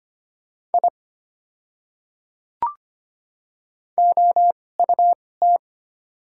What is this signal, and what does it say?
Morse code, heard as a keyed tone, about 25 wpm, spaced as for 18 wpm, 690 Hz. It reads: I OUT
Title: 2x Character Spacing - Top 200 Words - 25wpm